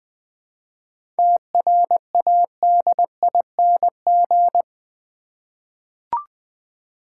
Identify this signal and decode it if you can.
Morse code, heard as a keyed tone, about 20 wpm, 700 Hz. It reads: TRADING
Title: 7 Letter Words - First 3 Letters Given - 20wpm